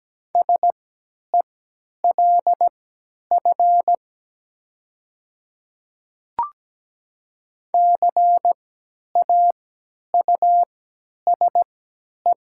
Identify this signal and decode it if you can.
Morse code, heard as a keyed tone, about 17 wpm, spaced as for 10 wpm, 700 Hz. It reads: SELF CAUSE
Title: 3x Character Spacing - Top 300 Words - 17wpm